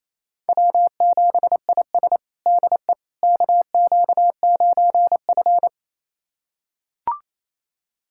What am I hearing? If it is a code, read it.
Morse code, heard as a keyed tone, about 28 wpm, 700 Hz. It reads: W7IS DE KQ9F
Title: USA Callsign DE Callsign - 4 Characters - 28wpm